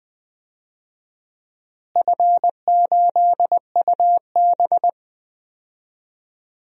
Morse code, heard as a keyed tone, about 20 wpm, 700 Hz. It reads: F8UB
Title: DXCC-100 Call Signs - 4 Characters - Repeated 2x - 20wpm